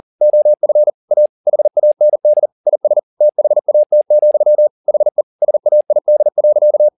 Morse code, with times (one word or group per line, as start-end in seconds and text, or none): 0.2-0.9 OF
1.1-1.3 A
1.5-2.5 HAND
2.7-3.0 IS
3.2-4.7 THAT,
4.9-5.2 HE
5.4-6.9 SAID.